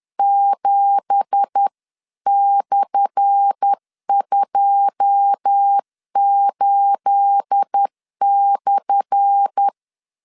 Morse code, heard as a keyed tone, about 10 wpm, 800 Hz. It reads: 7/28/
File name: LEZIONI / 10 SIGNS / 10A SLASH 10wpm 2808